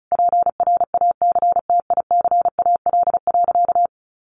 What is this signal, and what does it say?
Morse code, heard as a keyed tone, about 34 wpm, 700 Hz. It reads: PRACTICAL.
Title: Sentences from Top 1400 Words 35wpm